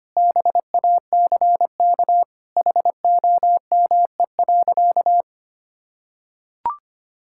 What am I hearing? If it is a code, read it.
Morse code, heard as a keyed tone, about 25 wpm, 700 Hz. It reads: BACK HOME.
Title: Sentences from Top 1000 Words - Review Part 6 - 25wpm